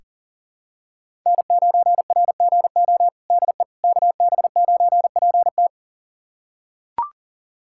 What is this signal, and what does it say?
Morse code, heard as a keyed tone, about 37 wpm, 700 Hz. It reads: N9RGO DE KB9PT
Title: USA Callsign DE Callsign - 5 Characters - 40wpm